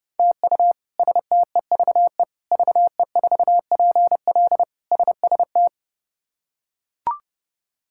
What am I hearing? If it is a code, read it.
Morse code, heard as a keyed tone, about 32 wpm, 700 Hz. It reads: TU STEVE VE4PL SST
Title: K1USN SST QSO Practice - 30wpm